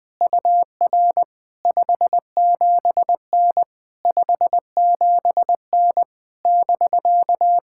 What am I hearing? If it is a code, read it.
Morse code, heard as a keyed tone, about 20 wpm, 700 Hz. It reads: UR 57N 57N <BK>
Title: POTA QSO Practice - 20wpm